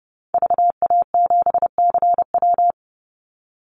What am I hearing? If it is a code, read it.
Morse code, heard as a keyed tone, about 29 wpm, 700 Hz. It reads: VA7CW
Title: Call Signs Repeated 2x - 30wpm